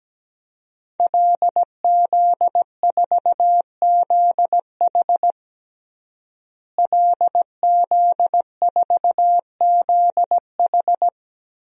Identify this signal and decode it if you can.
Morse code, heard as a keyed tone, about 17 wpm, 700 Hz. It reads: LZ4ZH LZ4ZH
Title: DXCC-100 Call Signs - 5 Characters - Repeated 3x - 17wpm